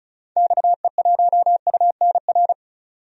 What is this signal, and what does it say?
Morse code, heard as a keyed tone, about 34 wpm, 700 Hz. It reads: XE1UNR
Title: DXCC-100 Call Signs - 6 Characters - 35wpm